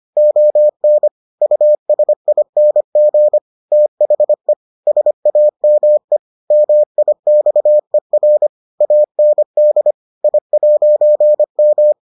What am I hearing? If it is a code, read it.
Morse code, heard as a keyed tone, about 25 wpm, 600 Hz. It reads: ON USING THE SAME MIXER AND I'M